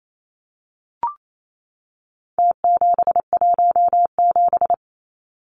Morse code, heard as a keyed tone, about 28 wpm, 700 Hz. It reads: T717
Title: Numbers - Contest Style - 4-Digit 28wpm